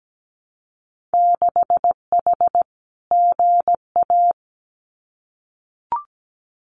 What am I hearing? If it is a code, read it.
Morse code, heard as a keyed tone, about 17 wpm, 700 Hz. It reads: 6H GA
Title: Winter Field Day Exchanges Encore 17wpm